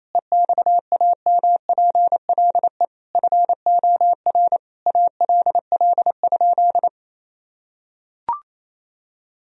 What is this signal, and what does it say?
Morse code, heard as a keyed tone, about 28 wpm, 700 Hz. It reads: EXAMPLE FOR ALL?
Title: Sentences from Top 1000 Words - Review Part 3 - 28wpm